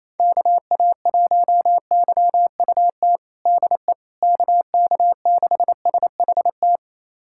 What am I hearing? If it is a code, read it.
Morse code, heard as a keyed tone, about 28 wpm, 700 Hz. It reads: KA1YUT DE KK6SHT
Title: USA Callsign DE Callsign 28wpm